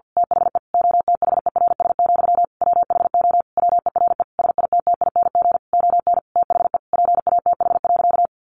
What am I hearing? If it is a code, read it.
Morse code, heard as a keyed tone, about 48 wpm, 700 Hz. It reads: THE OTHERS, WHO WERE SITTING ON THE PATH.
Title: A Sound Of Thunder 50wpm